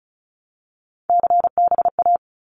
Morse code, heard as a keyed tone, about 36 wpm, 700 Hz. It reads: CBA
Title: Abbreviations 35wpm